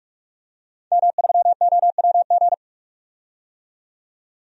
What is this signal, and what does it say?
Morse code, heard as a keyed tone, about 46 wpm, 700 Hz. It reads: M3OWG
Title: DXCC-100 Call Signs - 5 Characters - 45wpm